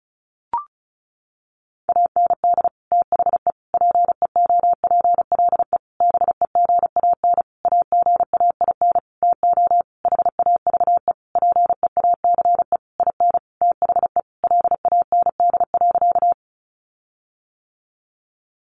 Morse code, heard as a keyed tone, about 34 wpm, 700 Hz. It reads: AND THE PEOPLE BEGAN AGAIN TO HAVE PEACE IN THE LAND.
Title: Sentences from Top 1500 Words 35wpm